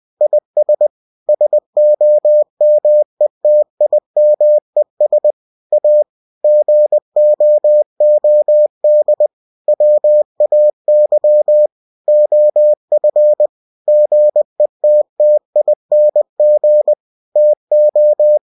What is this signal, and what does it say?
Morse code, heard as a keyed tone, about 20 wpm, 600 Hz. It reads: IS SOMETIMES A GOOD WAY OF GETTING TO
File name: cw-books / winnie-the-pooh / winnie-20wpm-page-0039